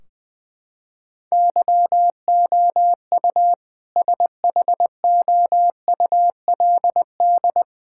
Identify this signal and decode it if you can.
Morse code, heard as a keyed tone, about 20 wpm, 700 Hz. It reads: YOU SHOULD
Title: Sentences from Top 1600 Words 20wpm